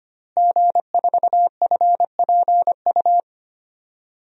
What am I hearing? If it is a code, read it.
Morse code, heard as a keyed tone, about 25 wpm, 700 Hz. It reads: G4FPU
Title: DXCC-100 Call Signs - 5 Characters - Repeated 2x - 25wpm